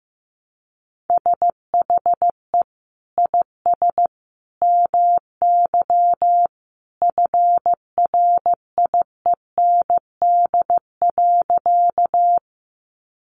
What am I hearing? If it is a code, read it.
Morse code, heard as a keyed tone, about 15 wpm, 700 Hz. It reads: SHE IS MY FRIEND.
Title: Sentences from Top 300 Words 15wpm